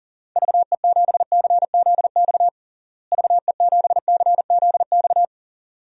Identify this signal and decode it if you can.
Morse code, heard as a keyed tone, about 40 wpm, 700 Hz. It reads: VE7CZX VE7CZX
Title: DXCC-100 Call Signs - 6 Characters - Repeated 2x - 40wpm